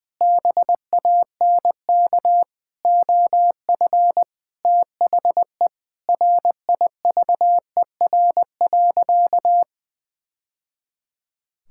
Morse code, heard as a keyed tone, about 20 wpm, 700 Hz. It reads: BANK OF THE RIVER.
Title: Sentences from Top 1000 Words - Review Part 6 - 20wpm